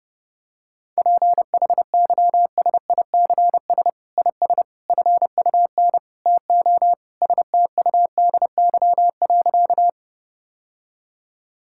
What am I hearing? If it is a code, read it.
Morse code, heard as a keyed tone, about 30 wpm, 700 Hz. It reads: PHYSICS IS FUN TO STUDY.
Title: Sentences from Top 1000 Words - Review Part 5 - 30wpm